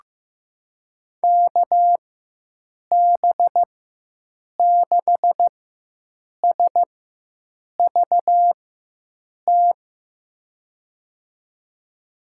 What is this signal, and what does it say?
Morse code, heard as a keyed tone, about 15 wpm, spaced as for 7 wpm, 700 Hz. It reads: KB6SVT
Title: USA Call Signs - 6 Characters - 4x Character Spacing - 15wpm